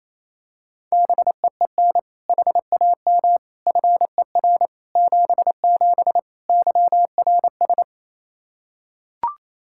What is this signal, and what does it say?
Morse code, heard as a keyed tone, about 27 wpm, 700 Hz. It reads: BEEN HAM FER 77 YRS
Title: Practice QSO - Exchange 3 - More Info - 28wpm